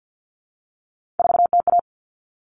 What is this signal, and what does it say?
Morse code, heard as a keyed tone, about 50 wpm, 700 Hz. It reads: 4TA
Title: All Cut Numbers - 3-Digit 50wpm